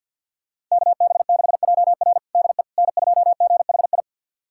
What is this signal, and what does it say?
Morse code, heard as a keyed tone, about 52 wpm, 700 Hz. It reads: KB6JR DE N2GHI
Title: USA Callsign DE Callsign 50wpm